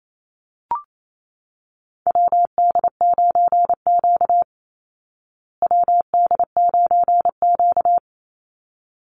Morse code, heard as a keyed tone, about 28 wpm, 700 Hz. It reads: WD9Q WD9Q
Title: Call Signs Repeated 3x - 28wpm